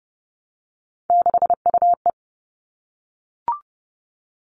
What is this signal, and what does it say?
Morse code, heard as a keyed tone, about 30 wpm, 700 Hz. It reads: 6UE
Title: All Cut Numbers - 3-Digit 30wpm